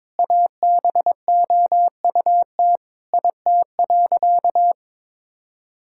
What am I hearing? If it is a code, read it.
Morse code, heard as a keyed tone, about 22 wpm, 700 Hz. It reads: ABOUT IT.
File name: practice / 22wpm / Sentences with 4-Word Phrases - 22wpm